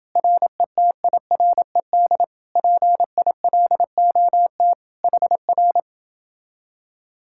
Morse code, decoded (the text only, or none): RETIRED PILOT HR